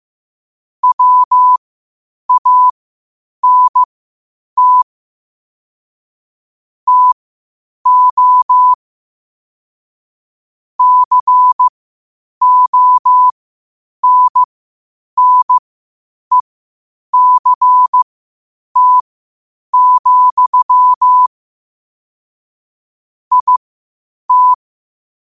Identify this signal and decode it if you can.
Morse code, heard as a keyed tone, about 15 wpm, spaced as for 8 wpm, 1000 Hz. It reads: WANT TO CONNECT, IT
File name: qotdfiles / 8wpm